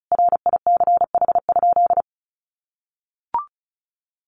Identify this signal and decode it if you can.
Morse code, heard as a keyed tone, about 34 wpm, 700 Hz. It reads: RICH?
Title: Sentences from Top 1100 Words 35wpm